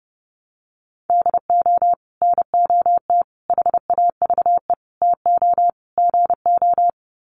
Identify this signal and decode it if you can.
Morse code, heard as a keyed tone, about 28 wpm, 700 Hz. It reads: DO NOT HAVE TO GO
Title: Sets of 5 Words 30wpm